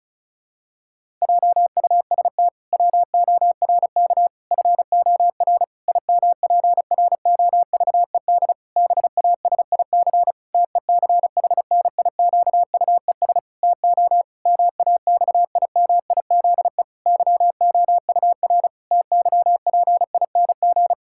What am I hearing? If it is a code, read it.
Morse code, heard as a keyed tone, about 36 wpm, 700 Hz. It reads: JUST WORK FOR IMPROVED BASIC TECHNIQUES TO MAXIMIZE YOUR TYPING
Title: Pangram Challenge v2 - 35wpm